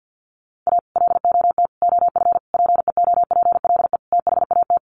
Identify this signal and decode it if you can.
Morse code, heard as a keyed tone, about 50 wpm, 700 Hz. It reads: A LOT OF PEOPLE THAT
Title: Sets of 5 Words Encore2 50wpm